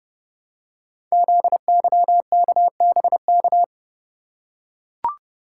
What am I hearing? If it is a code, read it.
Morse code, heard as a keyed tone, about 30 wpm, 700 Hz. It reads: ZYKBK
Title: Code Groups - 5 Letters - 30wpm